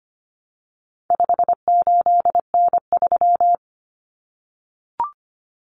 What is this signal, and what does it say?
Morse code, heard as a keyed tone, about 25 wpm, 700 Hz. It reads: 58N3
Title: Common Cut Numbers - 4-Digit 25wpm